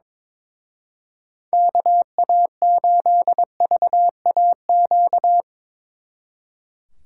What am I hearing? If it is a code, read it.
Morse code, heard as a keyed tone, about 24 wpm, 700 Hz. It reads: KA8VAQ
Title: Speed Racing - USA 2x3 Call Signs 22wpm